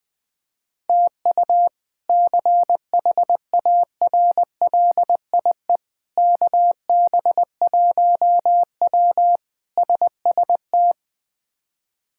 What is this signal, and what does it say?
Morse code, heard as a keyed tone, about 21 wpm, 700 Hz. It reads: TU CHARLIE KB1W SST
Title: K1USN SST QSO Practice - 20wpm